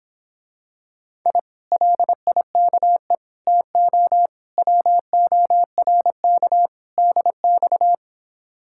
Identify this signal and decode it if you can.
Morse code, heard as a keyed tone, about 26 wpm, 700 Hz. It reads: I LIKE TO WORK DX